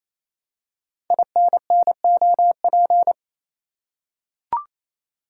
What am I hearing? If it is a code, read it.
Morse code, heard as a keyed tone, about 27 wpm, 700 Hz. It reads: INNOP
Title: Code Groups - 5 Letters - 28wpm